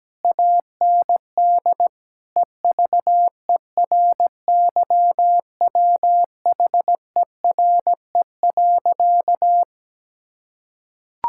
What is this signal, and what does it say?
Morse code, heard as a keyed tone, about 17 wpm, 700 Hz. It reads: AND EVERYWHERE.
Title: Sentences from Top 1600 Words 17wpm